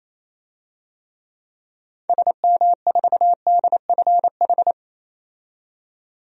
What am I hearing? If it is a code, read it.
Morse code, heard as a keyed tone, about 28 wpm, 700 Hz. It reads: SM4DFH
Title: DXCC-100 Call Signs - 6 Characters - 28wpm